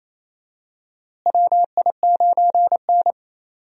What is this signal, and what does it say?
Morse code, heard as a keyed tone, about 28 wpm, 700 Hz. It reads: WI9N
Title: USA Call Signs - 4 Characters - Repeated 3x - 28wpm